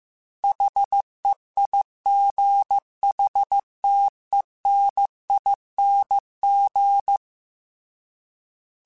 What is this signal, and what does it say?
Morse code, heard as a keyed tone, about 15 wpm, 800 Hz. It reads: HEIGHTENING